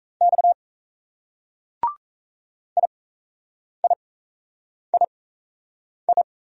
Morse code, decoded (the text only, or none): X IIII